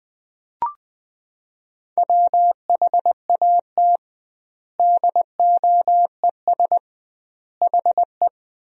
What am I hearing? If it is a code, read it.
Morse code, heard as a keyed tone, about 20 wpm, 700 Hz. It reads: WHAT DOES HE
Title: 2x Word-Spacing - Sentences from Top 400 Words - 20wpm